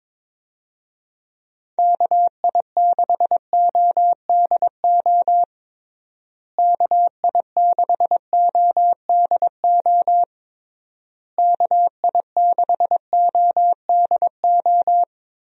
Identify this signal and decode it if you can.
Morse code, heard as a keyed tone, about 22 wpm, 700 Hz. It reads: KI6ODO KI6ODO KI6ODO
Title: USA Call Signs - 6 Characters - Repeated 3x - 22wpm